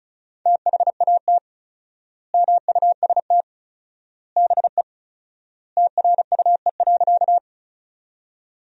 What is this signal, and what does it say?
Morse code, heard as a keyed tone, about 36 wpm, 700 Hz. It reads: THAT MUST BE TRUE.
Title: 4x Word-Spacing - Sentences from Top 400 Words - 35wpm